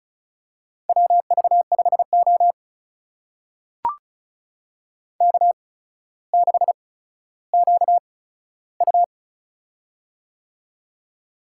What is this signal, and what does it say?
Morse code, heard as a keyed tone, about 36 wpm, spaced as for 15 wpm, 700 Hz. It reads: WV5O K6QU
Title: USA Call Signs - 4 Characters - 8x Character Spacing - 35wpm